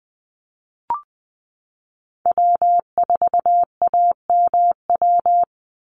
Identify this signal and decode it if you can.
Morse code, heard as a keyed tone, about 20 wpm, 700 Hz. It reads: W4AMW
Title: NA SOTA Chaser Callsigns 20wpm